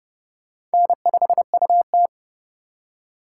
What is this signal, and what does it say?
Morse code, heard as a keyed tone, about 29 wpm, 700 Hz. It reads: N5UT